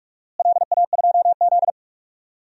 Morse code, heard as a keyed tone, about 46 wpm, 700 Hz. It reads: LA2Z
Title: DXCC-100 Call Signs - 4 Characters - Repeated 3x - 45wpm